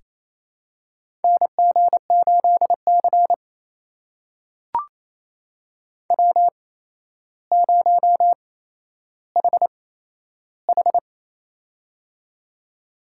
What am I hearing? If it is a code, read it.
Morse code, heard as a keyed tone, about 28 wpm, spaced as for 12 wpm, 700 Hz. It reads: NG8C W0HH
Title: USA Call Signs - 4 Characters - 8x Character Spacing - 28wpm